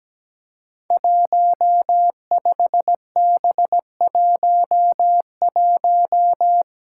Morse code, heard as a keyed tone, about 17 wpm, 700 Hz. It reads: 15B11
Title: Five Challenging Letter-Number Combinations 17wpm